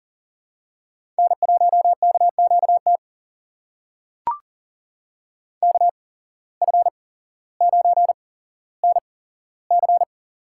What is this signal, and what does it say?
Morse code, heard as a keyed tone, about 40 wpm, spaced as for 17 wpm, 700 Hz. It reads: N1KQT KF9NC
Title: USA Call Signs - 5 Characters - 8x Character Spacing - 40wpm